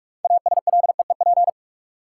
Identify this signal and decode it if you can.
Morse code, heard as a keyed tone, about 43 wpm, 700 Hz. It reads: ASLEEP
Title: Six Letter Words - 45wpm